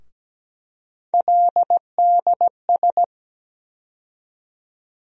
Morse code, heard as a keyed tone, about 17 wpm, 700 Hz. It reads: LDS
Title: USA Call Sign Postfixes - 3 Letters 17wpm